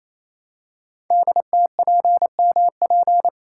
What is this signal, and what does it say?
Morse code, heard as a keyed tone, about 27 wpm, 700 Hz. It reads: DTPMP
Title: Code Groups - 5 Letters - 28wpm